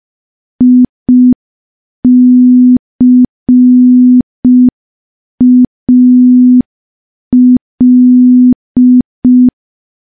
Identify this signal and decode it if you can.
Morse code, heard as a keyed tone, about 5 wpm, 250 Hz. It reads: ICAL
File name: morse / morse-training-05WPM